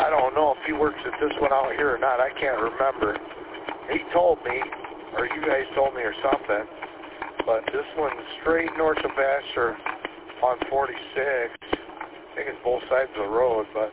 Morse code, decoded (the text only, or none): none